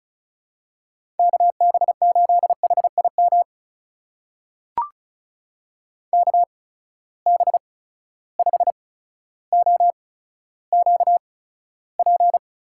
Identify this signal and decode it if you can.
Morse code, heard as a keyed tone, about 36 wpm, spaced as for 15 wpm, 700 Hz. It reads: KB8HIM KB5OQP
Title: USA Call Signs - 6 Characters - 8x Character Spacing - 35wpm